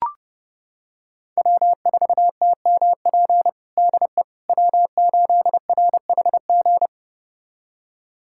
Morse code, heard as a keyed tone, about 30 wpm, 700 Hz. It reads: W4TMP DE W8RHG
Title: USA Callsign DE Callsign - 5 Characters - 30wpm